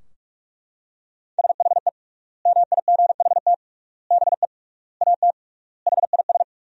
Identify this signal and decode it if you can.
Morse code, heard as a keyed tone, about 41 wpm, 700 Hz. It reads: SHE MIGHT BE AT HIS
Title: 3x Word-Spacing - Sentences from Top 300 Words - 45wpm